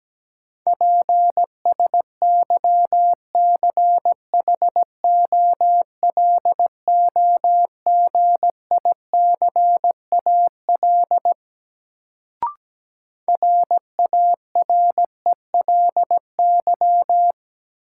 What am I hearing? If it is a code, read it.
Morse code, heard as a keyed tone, about 17 wpm, 700 Hz. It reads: PSYCHOLOGICAL RARELY
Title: Sentences from Top 1400 Words - New Word Drill - 17wpm